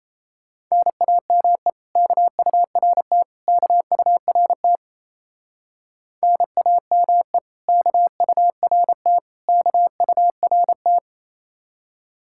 Morse code, heard as a keyed tone, about 33 wpm, 700 Hz. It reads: NAME KURT KURT NAME KURT KURT
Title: Speed-Racing Common QSO Phrases - 28wpm